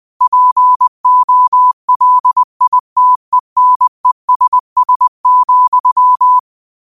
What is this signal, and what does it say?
Morse code, heard as a keyed tone, about 20 wpm, 1000 Hz. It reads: POLITENESS,